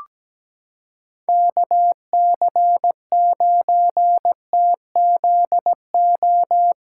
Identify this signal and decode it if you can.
Morse code, heard as a keyed tone, about 17 wpm, 700 Hz. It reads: KC9TZO
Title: K1USN SST QSO Practice - 17wpm